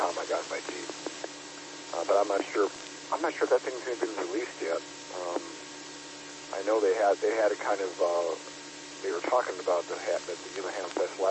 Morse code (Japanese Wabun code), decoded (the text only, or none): none